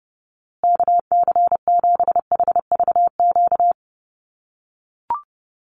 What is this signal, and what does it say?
Morse code, heard as a keyed tone, about 30 wpm, 700 Hz. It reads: KC7HVQ
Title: USA Call Signs Repeated 3x 30wpm